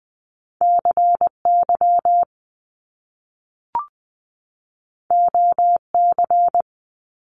Morse code, heard as a keyed tone, about 20 wpm, 700 Hz. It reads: CY OC
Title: Top Two Letter Combinations in Words 20wpm